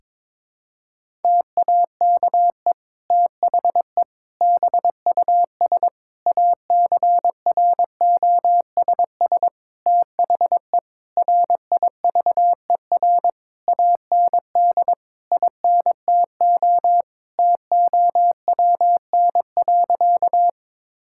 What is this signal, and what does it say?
Morse code, heard as a keyed tone, about 22 wpm, 700 Hz. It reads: TAKE THE BUS ACROSS THE RIVER AND INTO TOWN.
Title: Sentences from Top 1000 Words - Review Part 6 - 22wpm